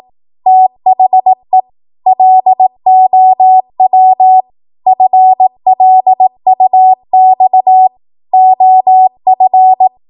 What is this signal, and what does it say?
Morse code, heard as a keyed tone, about 18 wpm, 750 Hz. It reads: THE LOW FLUX OF
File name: Morse / 18WPM